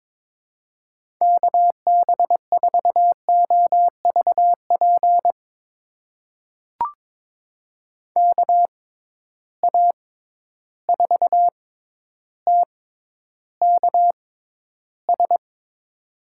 Morse code, heard as a keyed tone, about 21 wpm, spaced as for 11 wpm, 700 Hz. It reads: KB4OVP KA4TKS
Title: USA Call Signs - 6 Characters - 6x Character Spacing - 22wpm